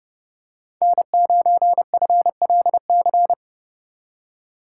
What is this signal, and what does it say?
Morse code, heard as a keyed tone, about 32 wpm, 700 Hz. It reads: N9FLC
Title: USA Call Signs - 5 Characters - 3x Character Spacing - 30wpm